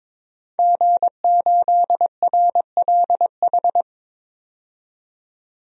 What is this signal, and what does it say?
Morse code, heard as a keyed tone, about 21 wpm, 700 Hz. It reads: G8RLH
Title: DXCC-100 Call Signs - 5 Characters - 22wpm